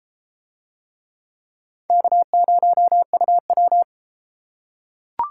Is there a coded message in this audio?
Yes: Morse code from 1.9 to 3.8 s, heard as a keyed tone, about 33 wpm, 700 Hz. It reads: K0UW